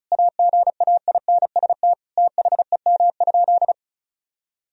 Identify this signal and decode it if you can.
Morse code, heard as a keyed tone, about 36 wpm, 700 Hz. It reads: AGAINST THEM?
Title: Sentences from Top 1000 Words - Review Part 2 - 35wpm